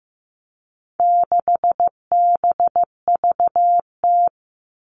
Morse code, heard as a keyed tone, about 15 wpm, 700 Hz. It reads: 6BVT
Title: All Cut Numbers - 4-Digit 15wpm